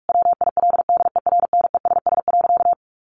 Morse code, heard as a keyed tone, about 44 wpm, 700 Hz. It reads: WILDERNESS.
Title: A Sound Of Thunder 45wpm